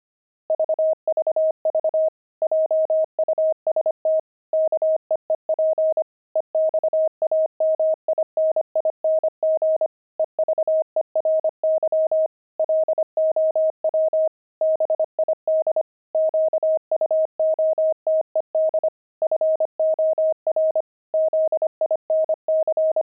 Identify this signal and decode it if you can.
Morse code, heard as a keyed tone, about 25 wpm, 630 Hz. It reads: VVV JUST KEEP EXAMINING EVERY LOW BID QUOTED FOR ZINC
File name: pangram-fs-25-cs-25